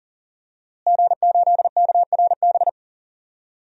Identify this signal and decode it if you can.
Morse code, heard as a keyed tone, about 40 wpm, 700 Hz. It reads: G8KRB
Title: DXCC-100 Call Signs - 5 Characters - Repeated 2x - 40wpm